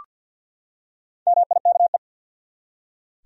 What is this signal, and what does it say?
Morse code, heard as a keyed tone, about 52 wpm, 700 Hz. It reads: MIKE
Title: USA Common Names - Max 4 Letters - 50wpm